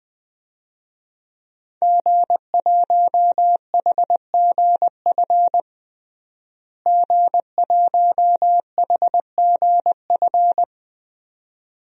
Morse code, heard as a keyed tone, about 21 wpm, 700 Hz. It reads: G1HGF G1HGF
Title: DXCC-100 Call Signs - 5 Characters - Repeated 2x - 20wpm